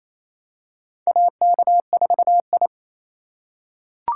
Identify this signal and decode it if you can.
Morse code, heard as a keyed tone, about 28 wpm, 700 Hz. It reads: AK4I
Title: USA Call Signs - 4 Characters - 3x Character Spacing - 28wpm